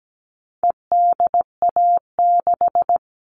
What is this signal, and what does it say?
Morse code, heard as a keyed tone, about 17 wpm, 700 Hz. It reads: EDA6
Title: All Cut Numbers - 4-Digit 17wpm